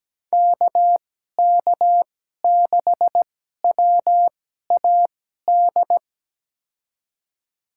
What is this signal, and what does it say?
Morse code, heard as a keyed tone, about 17 wpm, spaced as for 12 wpm, 700 Hz. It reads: KK6WAD
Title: USA Call Signs - 6 Characters - 2x Character Spacing - 17wpm